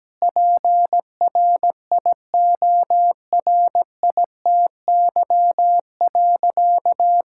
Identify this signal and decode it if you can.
Morse code, heard as a keyed tone, about 17 wpm, 700 Hz. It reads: PRIORITY.
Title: Sentences from Top 1600 Words 17wpm